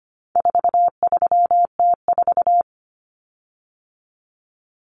Morse code, heard as a keyed tone, about 25 wpm, 700 Hz. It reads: 43T4
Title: Common Cut Numbers - 4-Digit 25wpm